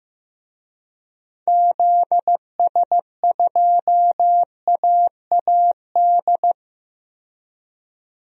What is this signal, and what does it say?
Morse code, heard as a keyed tone, about 15 wpm, 700 Hz. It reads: ZS2AAD
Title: DXCC-100 Call Signs - 6 Characters - 15wpm